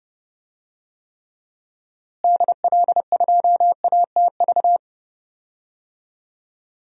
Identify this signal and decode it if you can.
Morse code, heard as a keyed tone, about 32 wpm, 700 Hz. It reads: DL2ATV
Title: DXCC-100 Call Signs - 6 Characters - Repeated 3x - 30wpm